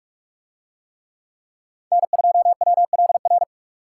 Morse code, heard as a keyed tone, about 41 wpm, 700 Hz. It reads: N2WLR